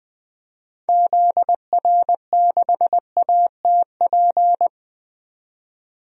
Morse code, heard as a keyed tone, about 20 wpm, 700 Hz. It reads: ZR6ATP
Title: DXCC-100 Call Signs - 6 Characters - 20wpm